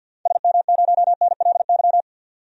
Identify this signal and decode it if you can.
Morse code, heard as a keyed tone, about 46 wpm, 700 Hz. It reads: SM0NLY
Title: DXCC-100 Call Signs - 6 Characters - 50wpm